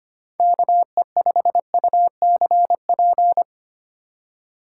Morse code, heard as a keyed tone, about 29 wpm, 700 Hz. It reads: KE5UCP
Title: Speed Racing - USA Call Signs 25wpm